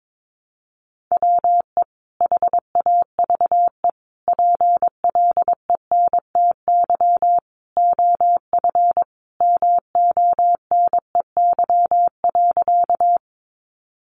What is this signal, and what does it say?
Morse code, heard as a keyed tone, about 22 wpm, 700 Hz. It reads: WE HAVE PLENTY OF MONEY.
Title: Sentences from Top 1300 Words 22wpm